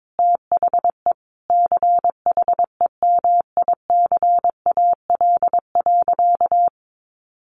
Morse code, heard as a keyed tone, about 22 wpm, 700 Hz. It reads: THE CHEMICAL.
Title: Sentences from Top 1200 Words 22wpm